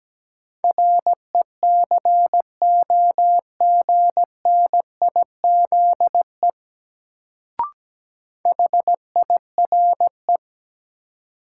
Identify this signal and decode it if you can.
Morse code, heard as a keyed tone, about 17 wpm, 700 Hz. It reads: RECOGNIZE HIRE
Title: Sentences from Top 1400 Words - New Word Drill - 17wpm